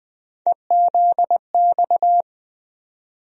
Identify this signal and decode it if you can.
Morse code, heard as a keyed tone, about 20 wpm, 700 Hz. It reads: EZX
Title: USA Call Sign Postfixes - 3 Letters 20wpm